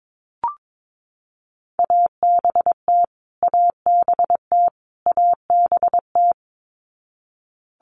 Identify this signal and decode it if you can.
Morse code, heard as a keyed tone, about 22 wpm, 700 Hz. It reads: ABT ABT ABT
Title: Abbreviations Repeated 3x - 22wpm